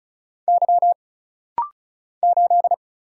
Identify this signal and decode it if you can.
Morse code, heard as a keyed tone, about 36 wpm, 700 Hz. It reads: Y 8
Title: Single Letter-Number - Rapid-Fire 35wpm